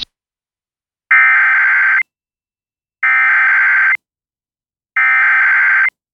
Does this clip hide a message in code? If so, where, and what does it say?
Yes: Emergency Alert System (SAME) tones (header bursts) from 1.1 to 5.9 s, heard as harsh buzzing bursts.